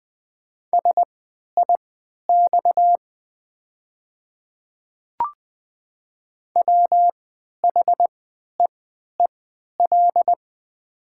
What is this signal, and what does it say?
Morse code, heard as a keyed tone, about 20 wpm, spaced as for 11 wpm, 700 Hz. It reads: SIX WHEEL
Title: 3x Character Spacing - Top 500 Words - 20wpm